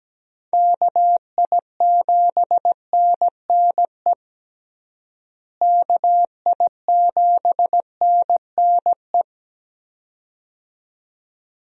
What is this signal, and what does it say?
Morse code, heard as a keyed tone, about 17 wpm, 700 Hz. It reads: KI7NNE KI7NNE
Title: USA Call Signs - 6 Characters - Repeated 2x - 17wpm